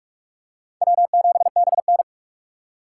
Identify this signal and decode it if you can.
Morse code, heard as a keyed tone, about 41 wpm, 700 Hz. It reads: W7BN
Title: USA Call Signs - 3x Character Spacing - 45wpm